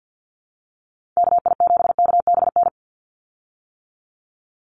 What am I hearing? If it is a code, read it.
Morse code, heard as a keyed tone, about 48 wpm, 700 Hz. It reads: KI7KBN
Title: USA Call Signs 50wpm